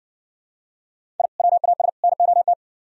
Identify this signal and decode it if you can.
Morse code, heard as a keyed tone, about 59 wpm, 700 Hz. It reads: I WAS NOT